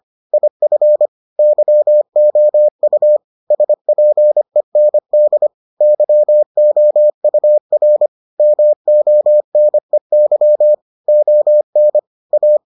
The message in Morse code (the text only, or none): IF YOU SPEND YOUR MONEY ON A